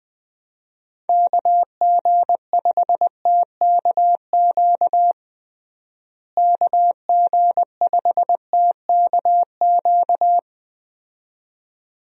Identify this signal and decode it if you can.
Morse code, heard as a keyed tone, about 20 wpm, 700 Hz. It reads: KG5TKQ KG5TKQ
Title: USA Call Signs - 6 Characters - Repeated 2x - 20wpm